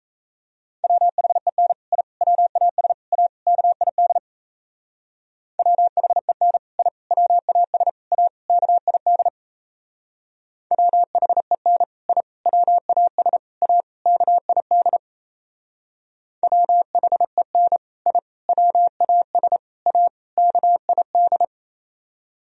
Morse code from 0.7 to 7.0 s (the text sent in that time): WHEN I WAS A KID WHEN I